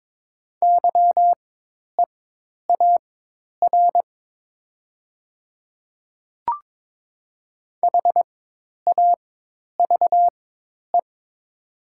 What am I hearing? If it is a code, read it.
Morse code, heard as a keyed tone, about 22 wpm, spaced as for 10 wpm, 700 Hz. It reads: YEAR HAVE
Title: 4x Character Spacing - Top 200 Words - 22wpm